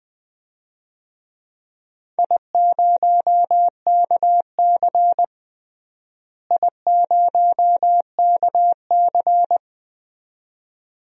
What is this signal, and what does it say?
Morse code, heard as a keyed tone, about 21 wpm, 700 Hz. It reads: I0KC I0KC
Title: DXCC-100 Call Signs - 4 Characters - Repeated 2x - 20wpm